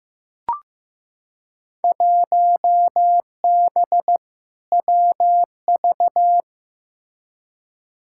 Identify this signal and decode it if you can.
Morse code, heard as a keyed tone, about 15 wpm, 700 Hz. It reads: 1B WV
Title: Field Day Exchanges Encore - 15wpm